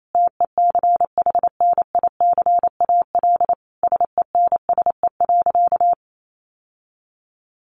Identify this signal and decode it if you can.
Morse code, heard as a keyed tone, about 28 wpm, 700 Hz. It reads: TECHNICAL SENSE.
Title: Sentences from Top 1100 Words 28wpm